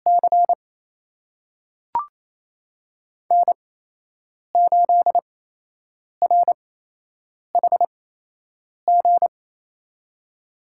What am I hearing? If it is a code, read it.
Morse code, heard as a keyed tone, about 28 wpm, spaced as for 8 wpm, 700 Hz. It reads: C N8RHG
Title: USA Call Signs - 8x Character Spacing - 28wpm